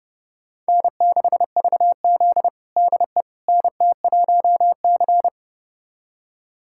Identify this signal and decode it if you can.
Morse code, heard as a keyed tone, about 30 wpm, 700 Hz. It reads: N6VZ DE NT1C
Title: USA Callsign DE Callsign - 4 Characters - 30wpm